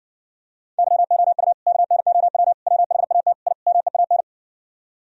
Morse code, heard as a keyed tone, about 60 wpm, 700 Hz. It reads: YOU KNOW WHAT I MEAN